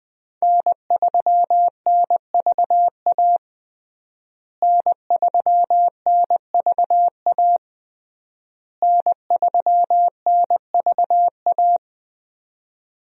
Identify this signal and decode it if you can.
Morse code, heard as a keyed tone, about 21 wpm, 700 Hz. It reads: N3NVA N3NVA N3NVA
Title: USA Call Signs - 5 Characters - Repeated 3x - 20wpm